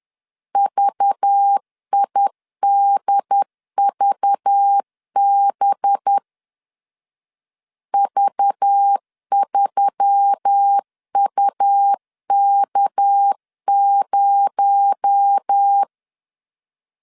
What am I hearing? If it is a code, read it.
Morse code, heard as a keyed tone, about 10 wpm, 800 Hz. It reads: VIDVB V3UK0
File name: CW / DET / 1 6 1500C S B 10wpm 4137